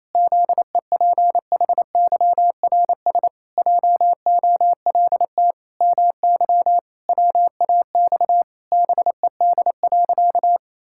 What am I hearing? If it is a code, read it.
Morse code, heard as a keyed tone, about 28 wpm, 700 Hz. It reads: ZEPHYRS JOLT MY WAX BED.
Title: Pangram Challenge v2 - 28wpm